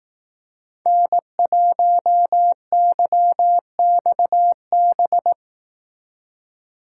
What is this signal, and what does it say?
Morse code, heard as a keyed tone, about 18 wpm, 700 Hz. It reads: N1YXB
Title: Speed Racing - USA 1x3 Call Signs 15wpm